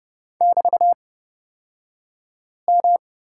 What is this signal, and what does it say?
Morse code, heard as a keyed tone, about 30 wpm, spaced as for 5 wpm, 690 Hz. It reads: <BT>M
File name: Lesson 36d - BT - Two Characters 30wpm